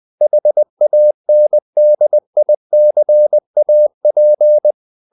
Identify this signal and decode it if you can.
Morse code, heard as a keyed tone, about 20 wpm, 600 Hz. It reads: HANDICAP
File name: the old man and the sea part 2 20wpm